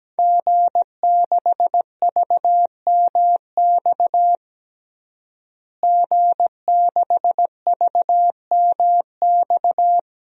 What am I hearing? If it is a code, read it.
Morse code, heard as a keyed tone, about 17 wpm, 700 Hz. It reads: G6VMX G6VMX
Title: DXCC-100 Call Signs - 5 Characters - Repeated 2x - 17wpm